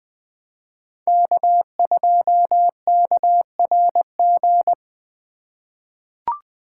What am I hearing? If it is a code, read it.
Morse code, heard as a keyed tone, about 20 wpm, 700 Hz. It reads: K2KRG
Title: USA Call Signs - 5 Characters - 3x Character Spacing - 20wpm